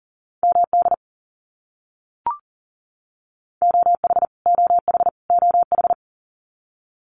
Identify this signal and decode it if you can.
Morse code, heard as a keyed tone, about 40 wpm, 700 Hz. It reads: MD OH OH OH
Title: US States Repeated 3x 40wpm